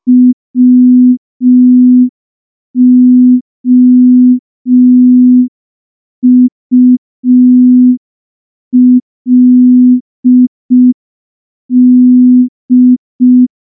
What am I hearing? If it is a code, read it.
Morse code, heard as a keyed tone, about 5 wpm, 250 Hz. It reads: WOULD
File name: morse / morse-training-05WPM